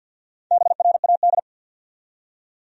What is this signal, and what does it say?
Morse code, heard as a keyed tone, about 53 wpm, 700 Hz. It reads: BRAD
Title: USA Common Names - Max 5 Letters - 50wpm